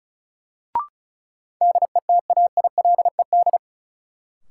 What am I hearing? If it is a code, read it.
Morse code, heard as a keyed tone, about 36 wpm, 700 Hz. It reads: DETAILED
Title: Sentences from Top 1500 Words - New Word Drill - 35wpm